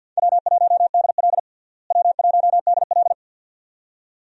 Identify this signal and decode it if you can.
Morse code, heard as a keyed tone, about 50 wpm, 700 Hz. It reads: W1DL W1DL
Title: USA Call Signs - 4 Characters - Repeated 2x - 50wpm